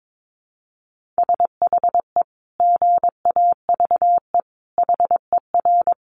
Morse code, heard as a keyed tone, about 22 wpm, 700 Hz. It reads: SHE GAVE HER
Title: Sentences from Top 1500 Words 22wpm